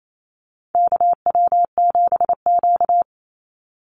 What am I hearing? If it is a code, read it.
Morse code, heard as a keyed tone, about 28 wpm, 700 Hz. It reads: KW7Q
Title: Call Signs Repeated 2x - 28wpm